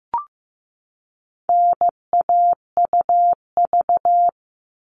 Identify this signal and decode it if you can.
Morse code, heard as a keyed tone, about 15 wpm, 700 Hz. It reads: NAUV
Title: All Cut Numbers - 4-Digit 15wpm